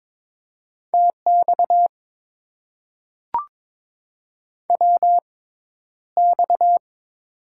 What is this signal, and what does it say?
Morse code, heard as a keyed tone, about 22 wpm, spaced as for 11 wpm, 700 Hz. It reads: TX WX
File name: Top 90 QSO Elements - 6x Character Spacing - 22wpm